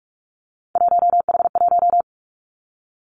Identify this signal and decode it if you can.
Morse code, heard as a keyed tone, about 46 wpm, 700 Hz. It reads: 1H1
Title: Three Challenging Letter-Number Combinations 45wpm